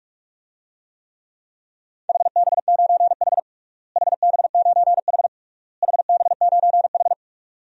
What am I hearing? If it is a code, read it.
Morse code, heard as a keyed tone, about 45 wpm, 690 Hz. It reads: HB9H HB9H HB9H
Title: DXCC-100 Call Signs - 4 Characters - Repeated 3x - 45wpm